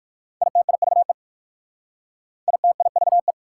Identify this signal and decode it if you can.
Morse code, heard as a keyed tone, about 60 wpm, 690 Hz. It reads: ITIVE ITIVE